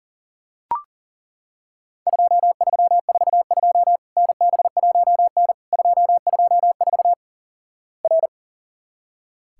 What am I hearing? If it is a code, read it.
Morse code, heard as a keyed tone, about 37 wpm, 700 Hz. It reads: 2342 NB1N 224 R
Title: Worked All Europe DX Contest - QTC Exchange - 40wpm